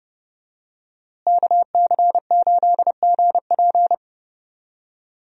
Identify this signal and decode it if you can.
Morse code, heard as a keyed tone, about 32 wpm, 700 Hz. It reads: KC8GP
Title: USA Call Signs - 5 Characters - Repeated 3x - 30wpm